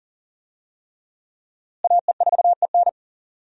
Morse code, heard as a keyed tone, about 40 wpm, 700 Hz. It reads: AE4EN
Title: USA Call Signs - 5 Characters - Repeated 3x - 40wpm